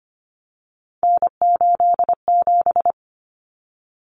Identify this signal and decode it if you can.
Morse code, heard as a keyed tone, about 25 wpm, 700 Hz. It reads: N87
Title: Common Cut Numbers - 3-Digit 25wpm